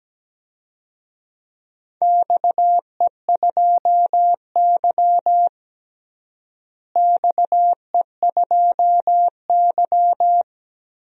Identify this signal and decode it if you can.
Morse code, heard as a keyed tone, about 17 wpm, 700 Hz. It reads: XE2Y XE2Y
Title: DXCC-100 Call Signs - 4 Characters - Repeated 3x - 17wpm